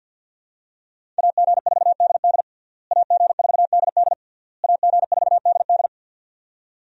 Morse code, heard as a keyed tone, about 48 wpm, 700 Hz. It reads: AG4DD AG4DD AG4DD
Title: USA Call Signs - 5 Characters - Repeated 3x - 50wpm